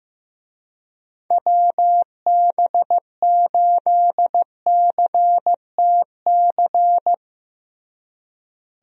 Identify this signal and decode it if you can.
Morse code, heard as a keyed tone, about 15 wpm, 700 Hz. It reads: WB8CTC
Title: USA Call Signs - 6 Characters - Repeated 3x - 15wpm